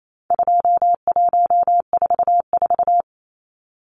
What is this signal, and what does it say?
Morse code, heard as a keyed tone, about 28 wpm, 700 Hz. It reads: 2144
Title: Numbers - 4-Digit 28wpm